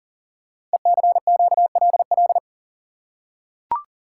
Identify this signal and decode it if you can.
Morse code, heard as a keyed tone, about 38 wpm, 700 Hz. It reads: ECQLL
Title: Code Groups - 5 Letters - 40wpm